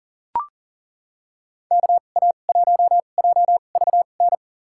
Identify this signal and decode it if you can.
Morse code, heard as a keyed tone, about 40 wpm, spaced as for 29 wpm, 700 Hz. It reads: KA1JVN
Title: USA Call Signs - 6 Characters - 2x Character Spacing - 40wpm